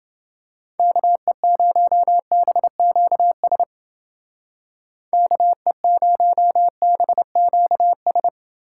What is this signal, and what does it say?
Morse code, heard as a keyed tone, about 29 wpm, 700 Hz. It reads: KE0BQS KE0BQS